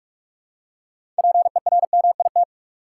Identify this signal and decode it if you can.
Morse code, heard as a keyed tone, about 43 wpm, 700 Hz. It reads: PERMIT
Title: Six Letter Words - 45wpm